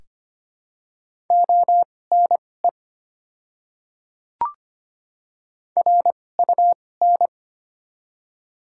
Morse code, heard as a keyed tone, about 25 wpm, spaced as for 18 wpm, 700 Hz. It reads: ONE RUN
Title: 2x Character Spacing - Top 500 Words - 25wpm